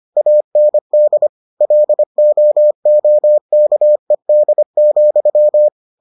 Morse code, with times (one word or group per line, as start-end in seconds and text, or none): 0.2-1.3 AND
1.6-5.7 LOOKED,